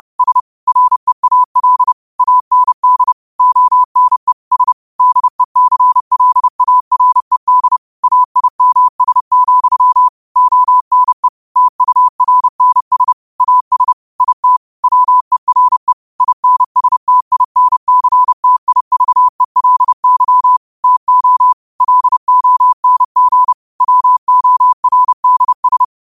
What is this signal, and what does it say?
Morse code, heard as a keyed tone, about 30 wpm, 1000 Hz. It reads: S REAL AND ONE S DECLARED AIMS, ONE TURNS AS IT WERE INSTINCTIVELY TO LONG WORDS
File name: qotdfiles / 30wpm